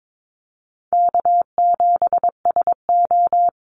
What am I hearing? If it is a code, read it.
Morse code, heard as a keyed tone, about 22 wpm, 700 Hz. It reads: K7SO
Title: NA SOTA Activator Callsigns 22wpm